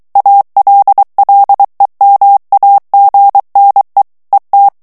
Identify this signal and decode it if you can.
Morse code, heard as a keyed tone, about 23 wpm, 800 Hz. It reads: ALLEMAGNE ET
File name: morse mp3 / FRANCAIS TEXTE CLAIR 1 VITESSE 22 MOTS MINUTES